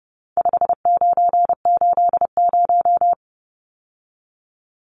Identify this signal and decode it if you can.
Morse code, heard as a keyed tone, about 30 wpm, 700 Hz. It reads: 5980
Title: Numbers - 4-Digit 30wpm